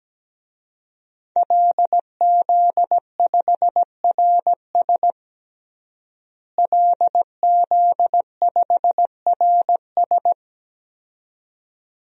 Morse code, heard as a keyed tone, about 17 wpm, 700 Hz. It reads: LZ5RS LZ5RS
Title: DXCC-100 Call Signs - 5 Characters - Repeated 2x - 17wpm